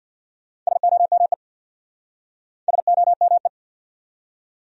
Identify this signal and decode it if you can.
Morse code, heard as a keyed tone, about 66 wpm, 700 Hz. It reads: SOME SOME